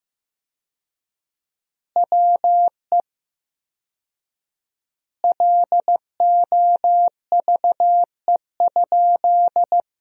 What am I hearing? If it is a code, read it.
Morse code, heard as a keyed tone, about 15 wpm, 700 Hz. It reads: WE LOVE?
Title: 4x Word-Spacing - Sentences from Top 400 Words - 15wpm